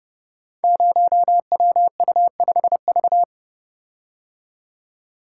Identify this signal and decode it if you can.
Morse code, heard as a keyed tone, about 30 wpm, 700 Hz. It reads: 0WU5V
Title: Code Groups - 5 Letter:Numbers - 30wpm